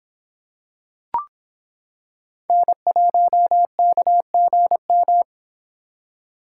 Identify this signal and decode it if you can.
Morse code, heard as a keyed tone, about 26 wpm, 700 Hz. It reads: N1KGM